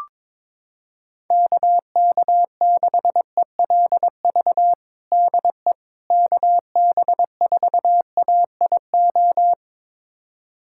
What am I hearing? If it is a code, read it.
Morse code, heard as a keyed tone, about 22 wpm, 700 Hz. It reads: KK6ELV DE KB4AIO
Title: USA Callsign DE Callsign - 6 Characters - 22wpm